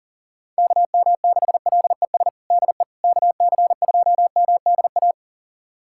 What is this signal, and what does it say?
Morse code, heard as a keyed tone, about 37 wpm, 700 Hz. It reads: KM6LES DE KC2MDA
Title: USA Callsign DE Callsign - 6 Characters - 40wpm